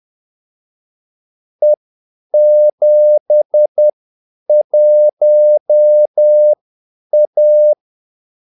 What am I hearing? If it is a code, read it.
Morse code, heard as a keyed tone, about 10 wpm, spaced as for 8 wpm, 600 Hz. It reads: E71A
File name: Callsigns-10-8wpm